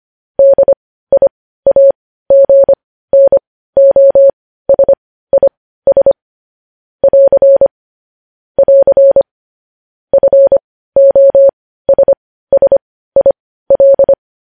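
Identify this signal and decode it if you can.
Morse code, heard as a keyed tone, about 25 wpm, spaced as for 15 wpm, 550 Hz. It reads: DIAGNOSIS <AR> <AR> FOSSIL
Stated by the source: Morse code news headlines practice at 15.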